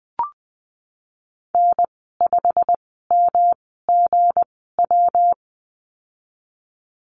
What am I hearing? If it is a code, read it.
Morse code, heard as a keyed tone, about 20 wpm, spaced as for 14 wpm, 700 Hz. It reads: N5MGW
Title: 173 Lesson 28g - Number 9 - Call Signs 20wpm